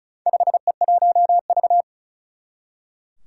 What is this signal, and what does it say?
Morse code, heard as a keyed tone, about 36 wpm, 700 Hz. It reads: 5E1V
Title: Four Challenging Letter-Number Combinations 35wpm